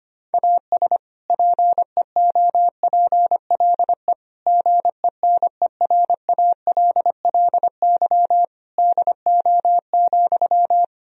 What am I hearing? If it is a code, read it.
Morse code, heard as a keyed tone, about 25 wpm, 700 Hz. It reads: AS PEOPLE GENERALLY DO,
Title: Sentences from Top 1000 Words - Review Part 4 - 25wpm